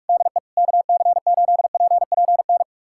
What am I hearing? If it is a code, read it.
Morse code, heard as a keyed tone, about 41 wpm, 700 Hz. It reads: DE KC8PPN
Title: USA Callsign DE Callsign - 6 Characters - 45wpm